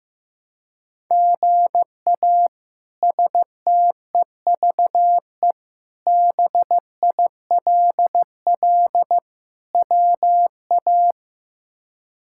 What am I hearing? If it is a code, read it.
Morse code, heard as a keyed tone, about 15 wpm, 700 Hz. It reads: GA STEVE BILL WA
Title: K1USN SST QSO Practice - 15wpm